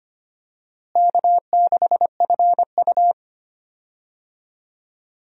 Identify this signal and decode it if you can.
Morse code, heard as a keyed tone, about 25 wpm, 700 Hz. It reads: K6FU
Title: USA Call Signs - 4 Characters - Repeated 3x - 25wpm